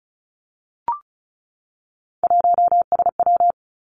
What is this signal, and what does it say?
Morse code, heard as a keyed tone, about 36 wpm, 700 Hz. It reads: 1SW
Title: Three Challenging Letter-Number Combinations 35wpm